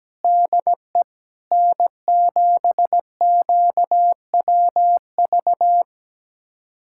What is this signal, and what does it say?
Morse code, heard as a keyed tone, about 17 wpm, 700 Hz. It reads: DE N7QWV
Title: USA Callsign DE Callsign - 5 Characters - 17wpm